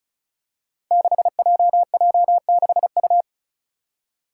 Morse code, heard as a keyed tone, about 34 wpm, 700 Hz. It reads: 6JJ6U
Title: Five Challenging Letter-Number Combinations 35wpm